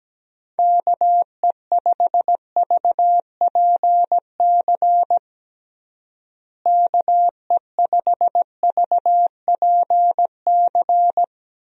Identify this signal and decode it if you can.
Morse code, heard as a keyed tone, about 17 wpm, 700 Hz. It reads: KE5VPC KE5VPC
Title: USA Call Signs - 6 Characters - Repeated 3x - 17wpm